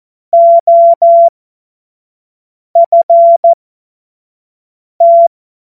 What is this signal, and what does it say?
Morse code, heard as a keyed tone, about 14 wpm, spaced as for 5 wpm, 680 Hz. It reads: OFT
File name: Morsekurs / Witze / Witz 16. November 2021, 25 F70